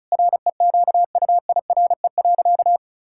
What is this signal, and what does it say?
Morse code, heard as a keyed tone, about 36 wpm, 700 Hz. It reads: REQUIRE.
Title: Sentences from Top 1000 Words - Review Part 6 - 35wpm